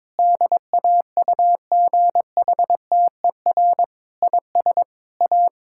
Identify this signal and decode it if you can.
Morse code, heard as a keyed tone, about 22 wpm, 700 Hz. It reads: DAUGHTER IS A
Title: Sentences from Top 1600 Words 22wpm